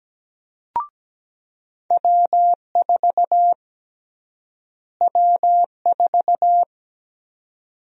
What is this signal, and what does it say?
Morse code, heard as a keyed tone, about 17 wpm, 700 Hz. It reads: W4 W4
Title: USA Call Sign Prefixes - Repeated 2x - 17wpm